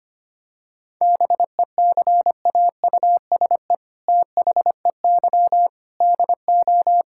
Morse code, heard as a keyed tone, about 25 wpm, 700 Hz. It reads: BECAUSE THEY DO
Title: Sentences from Top 1000 Words - Review Part 5 - 25wpm